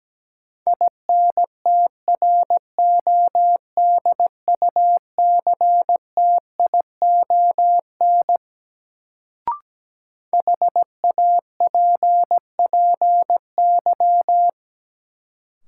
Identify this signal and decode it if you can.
Morse code, heard as a keyed tone, about 17 wpm, 700 Hz. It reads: INTRODUCTION HAPPY
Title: Sentences from Top 1000 Words - New Word Drill - 17wpm